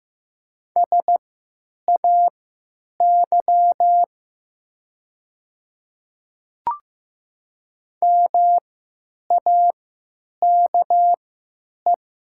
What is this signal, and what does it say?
Morse code, heard as a keyed tone, about 15 wpm, spaced as for 8 wpm, 700 Hz. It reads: SAY MAKE
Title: 3x Character Spacing - Top 200 Words - 15wpm